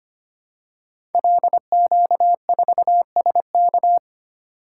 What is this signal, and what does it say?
Morse code, heard as a keyed tone, about 25 wpm, 700 Hz. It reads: LQ4SK
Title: Code Groups - 5 Letter:Numbers - 25wpm